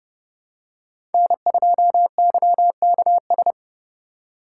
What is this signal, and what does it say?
Morse code, heard as a keyed tone, about 30 wpm, 700 Hz. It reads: N2YKS
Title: USA Call Signs - 5 Characters - 3x Character Spacing - 30wpm